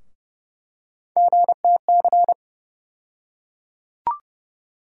Morse code, heard as a keyed tone, about 30 wpm, 700 Hz. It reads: GTC
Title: USA Call Sign Postfixes - 3 Letters 30wpm